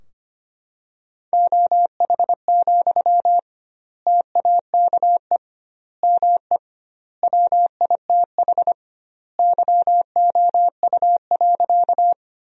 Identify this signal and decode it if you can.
Morse code, heard as a keyed tone, about 25 wpm, 700 Hz. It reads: OH, TAKE ME WITH YOU.
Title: 2x Word-Spacing - Sentences from Top 500 Words - 25wpm